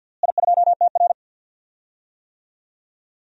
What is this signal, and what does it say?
Morse code, heard as a keyed tone, about 47 wpm, 700 Hz. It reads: I2TR
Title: DXCC-100 Call Signs - 4 Characters - Repeated 3x - 50wpm